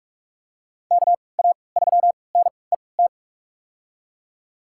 Morse code, heard as a keyed tone, about 46 wpm, spaced as for 25 wpm, 690 Hz. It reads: KA3NET
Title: USA Call Signs - 6 Characters - 3x Character Spacing - 45wpm